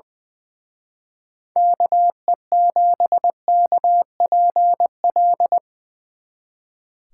Morse code, heard as a keyed tone, about 22 wpm, 700 Hz. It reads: KE7KPL
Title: Speed Racing - USA Call Signs 20wpm